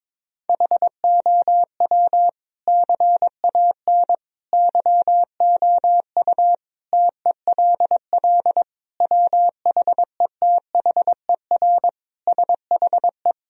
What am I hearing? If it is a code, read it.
Morse code, heard as a keyed tone, about 22 wpm, 700 Hz. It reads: HOW CAN YOU TELL WHETHER SHE
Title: Sentences from Top 1000 Words - Review Part 3 - 22wpm